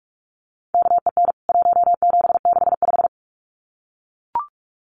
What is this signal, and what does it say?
Morse code, heard as a keyed tone, about 44 wpm, 700 Hz. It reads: KEN 1765
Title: CWT Exchange 45wpm